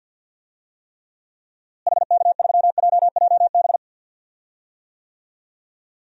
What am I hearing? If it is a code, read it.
Morse code, heard as a keyed tone, about 45 wpm, 700 Hz. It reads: HK3JJB
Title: DXCC-100 Call Signs - 6 Characters - Repeated 2x - 50wpm